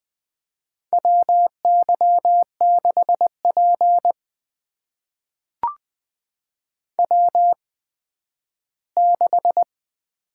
Morse code, heard as a keyed tone, about 20 wpm, spaced as for 12 wpm, 700 Hz. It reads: WY6P W6
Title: USA Call Signs - 4 Characters - 8x Character Spacing - 20wpm